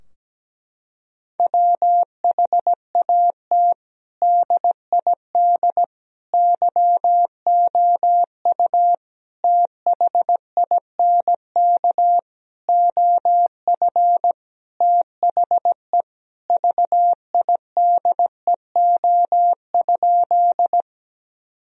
Morse code, heard as a keyed tone, about 17 wpm, 700 Hz. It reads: WHAT DID YOU THINK OF THE VIDEO?
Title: Sentences from Top 1000 Words - Review Part 4 - 17wpm